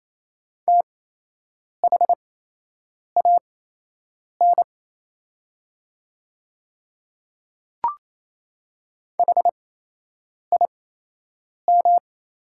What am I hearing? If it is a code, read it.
Morse code, heard as a keyed tone, about 28 wpm, spaced as for 8 wpm, 700 Hz. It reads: THAN HIM
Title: 8x Character Spacing - Top 100 Words - 28wpm